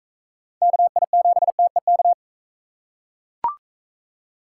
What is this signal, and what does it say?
Morse code, heard as a keyed tone, about 43 wpm, 700 Hz. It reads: KI7TEK